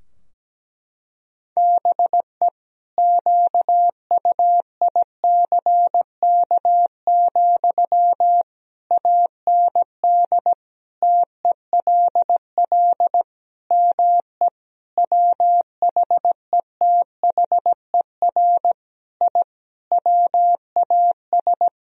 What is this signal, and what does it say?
Morse code, heard as a keyed tone, about 17 wpm, 700 Hz. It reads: BE QUICK, AND TELL ME WHETHER I WAS
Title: Sentences from Top 1000 Words - Review Part 3 - 17wpm